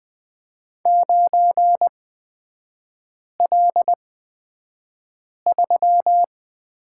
Morse code, heard as a keyed tone, about 20 wpm, spaced as for 5 wpm, 700 Hz. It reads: 9L3
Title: 255 Lesson 41e - Period - Three Characters 20wpm